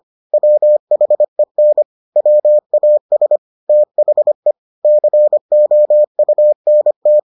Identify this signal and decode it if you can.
Morse code, heard as a keyed tone, about 25 wpm, 600 Hz. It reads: WHEN WAS THE COUNT